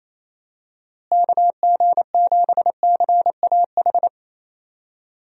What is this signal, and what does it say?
Morse code, heard as a keyed tone, about 28 wpm, 700 Hz. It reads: KG7CAH
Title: USA Call Signs - 6 Characters - Repeated 2x - 28wpm